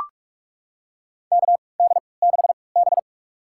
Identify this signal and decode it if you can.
Morse code, heard as a keyed tone, about 44 wpm, spaced as for 25 wpm, 700 Hz. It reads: KD6B
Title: USA Call Signs - 3x Character Spacing - 45wpm